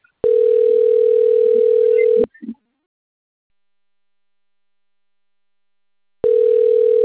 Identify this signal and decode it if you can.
Ringback tone (North America), 440 + 480 Hz.